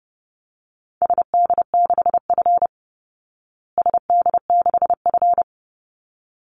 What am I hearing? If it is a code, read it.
Morse code, heard as a keyed tone, about 30 wpm, 700 Hz. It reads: SD6F SD6F
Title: Call Signs Repeated 3x - 30wpm